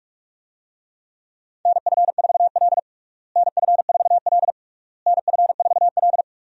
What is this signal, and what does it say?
Morse code, heard as a keyed tone, about 41 wpm, 700 Hz. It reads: NF4L NF4L NF4L
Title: USA Call Signs - 4 Characters - Repeated 3x - 45wpm